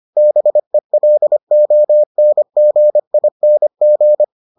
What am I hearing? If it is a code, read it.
Morse code, heard as a keyed tone, about 25 wpm, 600 Hz. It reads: BELONGING